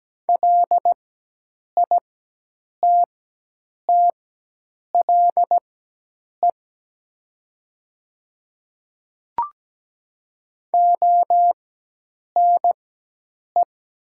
Morse code, heard as a keyed tone, about 17 wpm, spaced as for 8 wpm, 700 Hz. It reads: LITTLE ONE
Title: 4x Character Spacing - Top 200 Words - 17wpm